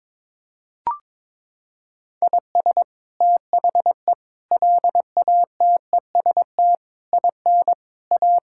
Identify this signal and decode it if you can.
Morse code, heard as a keyed tone, about 22 wpm, 700 Hz. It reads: IS THE LATEST IN A
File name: practice / 22wpm / Sets of 8 Words - 22wpm